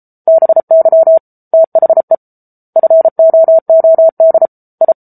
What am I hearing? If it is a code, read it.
Morse code, heard as a keyed tone, about 34 wpm, 650 Hz. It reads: BY THE FOOD I